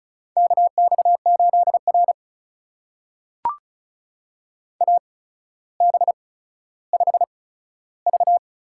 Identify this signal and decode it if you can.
Morse code, heard as a keyed tone, about 36 wpm, spaced as for 15 wpm, 700 Hz. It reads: KX8R AB5V
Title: USA Call Signs - 4 Characters - 8x Character Spacing - 35wpm